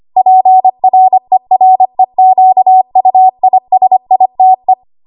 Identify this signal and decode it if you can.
Morse code, heard as a keyed tone, about 25 wpm, 750 Hz. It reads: PREREQUISITE
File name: Archive / 25 WPM / 25WPM